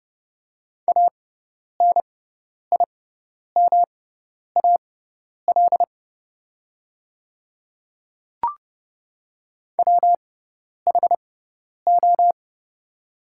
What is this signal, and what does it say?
Morse code, heard as a keyed tone, about 30 wpm, spaced as for 10 wpm, 700 Hz. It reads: ANIMAL WHO
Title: 6x Character Spacing - Top 300 Words - 30wpm